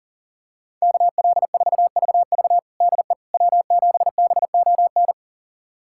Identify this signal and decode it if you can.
Morse code, heard as a keyed tone, about 37 wpm, 700 Hz. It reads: KL4VV DE W7BON
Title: USA Callsign DE Callsign - 5 Characters - 40wpm